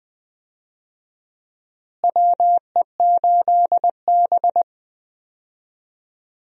Morse code, heard as a keyed tone, about 20 wpm, 700 Hz. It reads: WE8B
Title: USA Call Signs - 4 Characters - Repeated 2x - 20wpm